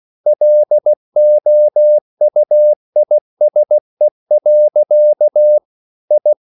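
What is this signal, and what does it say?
Morse code, heard as a keyed tone, about 16 wpm, 600 Hz. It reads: LOUISE. I
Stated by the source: Slow (QRS) Morse code at 16 wpm, from CW med Gnister.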